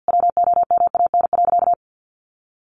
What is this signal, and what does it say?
Morse code, heard as a keyed tone, about 50 wpm, 700 Hz. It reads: WOMAN.
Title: Sentences from Top 1500 Words 50wpm